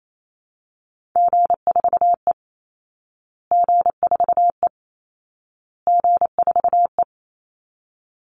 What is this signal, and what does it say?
Morse code, heard as a keyed tone, about 28 wpm, 700 Hz. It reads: G4E G4E G4E
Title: Call Signs Repeated 2x - 28wpm